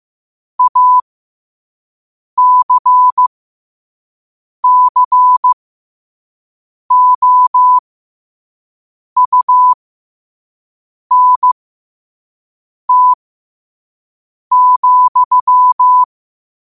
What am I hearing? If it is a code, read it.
Morse code, heard as a keyed tone, about 15 wpm, spaced as for 5 wpm, 1000 Hz. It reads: ACCOUNT,
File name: qotdfiles / 5wpm